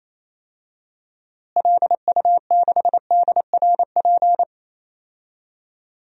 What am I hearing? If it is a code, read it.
Morse code, heard as a keyed tone, about 28 wpm, 700 Hz. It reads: LU6DRP
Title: DXCC-100 Call Signs - 6 Characters - Repeated 3x - 28wpm